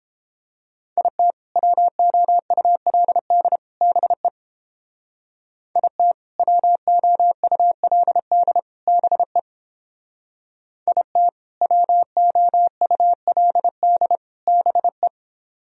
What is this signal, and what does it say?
Morse code, heard as a keyed tone, about 30 wpm, 700 Hz. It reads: IT WOULD BE IT WOULD BE IT WOULD BE